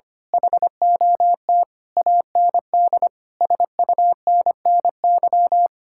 Morse code, heard as a keyed tone, about 24 wpm, 700 Hz. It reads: HOT AND SUNNY
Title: Speed-Racing Common QSO Phrases Encore - 25wpm